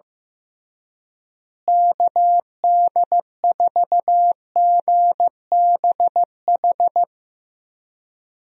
Morse code, heard as a keyed tone, about 15 wpm, 700 Hz. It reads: KD4GBH
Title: USA Call Signs - 6 Characters - Repeated 2x - 15wpm